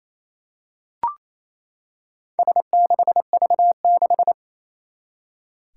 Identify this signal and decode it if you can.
Morse code, heard as a keyed tone, about 28 wpm, 700 Hz. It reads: S6V6
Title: Four Challenging Letter-Number Combinations 28wpm